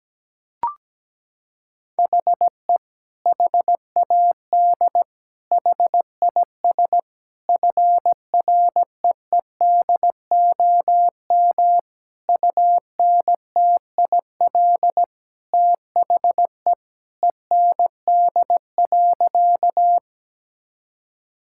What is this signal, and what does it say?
Morse code, heard as a keyed tone, about 17 wpm, 700 Hz. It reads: HE HAD HIS FREEDOM UNTIL THE END.
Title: Sentences from Top 1000 Words - Review Part 5 - 17wpm